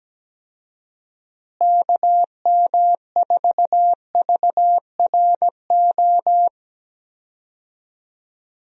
Morse code, heard as a keyed tone, about 17 wpm, 700 Hz. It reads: KM4VRO
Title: USA Call Signs - 6 Characters - Repeated 3x - 17wpm